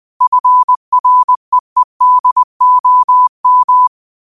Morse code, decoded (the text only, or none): FREEDOM